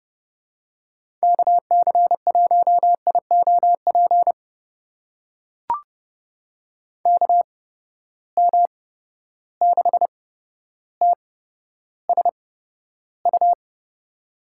Morse code, heard as a keyed tone, about 30 wpm, spaced as for 13 wpm, 700 Hz. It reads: KC1IOP KM6TSU
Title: USA Call Signs - 6 Characters - 8x Character Spacing - 30wpm